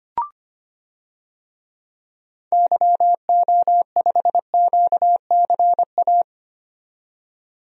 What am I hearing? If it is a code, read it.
Morse code, heard as a keyed tone, about 25 wpm, 700 Hz. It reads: YO5QCA
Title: DXCC-100 Call Signs - 6 Characters - 25wpm